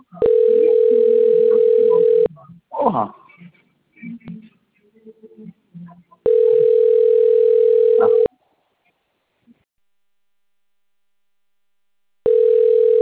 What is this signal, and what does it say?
Ringback tone (North America), 440 + 480 Hz, with speech.
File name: rg-1600-unknown- - - .67150